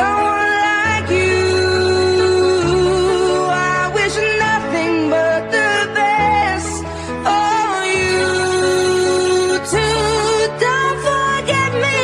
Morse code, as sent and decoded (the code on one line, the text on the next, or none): none